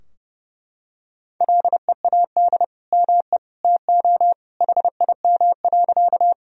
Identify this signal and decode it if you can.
Morse code, heard as a keyed tone, about 32 wpm, 700 Hz. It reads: LEAD ME TO HIM.
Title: Sentences from Top 1000 Words - Review Part 2 - 30wpm